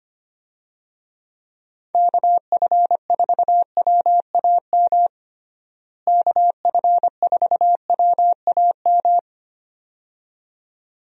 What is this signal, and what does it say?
Morse code, heard as a keyed tone, about 25 wpm, 700 Hz. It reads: KF4WAM KF4WAM
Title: USA Call Signs - 6 Characters - Repeated 2x - 25wpm